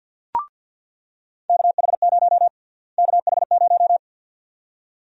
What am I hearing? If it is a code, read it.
Morse code, heard as a keyed tone, about 47 wpm, 700 Hz. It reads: KH0 KH0
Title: USA Call Sign Prefixes - Repeated 2x - 50wpm